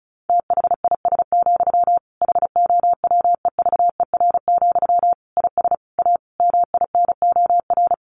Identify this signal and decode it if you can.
Morse code, heard as a keyed tone, about 34 wpm, 700 Hz. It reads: THIS, HOWEVER, IS A MINOR
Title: Sentences from Top 1500 Words 35wpm